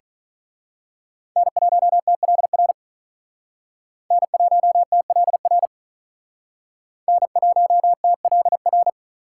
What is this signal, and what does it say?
Morse code, heard as a keyed tone, about 42 wpm, 700 Hz. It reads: N1TLR N1TLR N1TLR
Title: Speed Racing - USA 1x3 Call Signs 35wpm